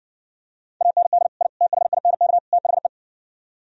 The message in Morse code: AND I THINK THE

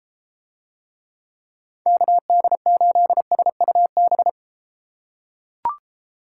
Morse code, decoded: KD8SUB

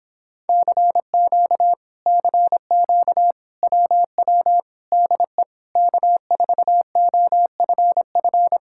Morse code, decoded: CQ CQ WW DE K4OFF

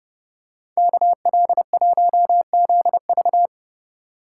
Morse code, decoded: KL1ZV

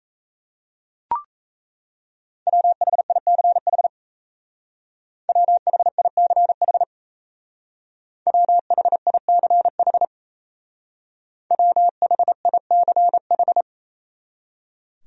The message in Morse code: WHICH WHICH WHICH WHICH

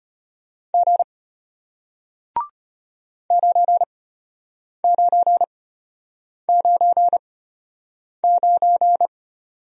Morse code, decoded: G 9999

